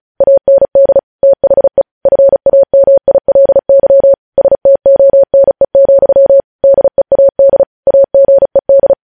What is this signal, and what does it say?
Morse code, heard as a keyed tone, about 36 wpm, 550 Hz. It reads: AND THE FAMILY STONE, DEAD AGED
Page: news headlines in morse code at 35 wpm